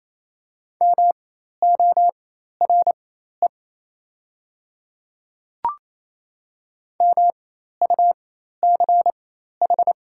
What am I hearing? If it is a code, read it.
Morse code, heard as a keyed tone, about 28 wpm, spaced as for 13 wpm, 700 Hz. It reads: MORE MUCH
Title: 4x Character Spacing - Top 300 Words - 28wpm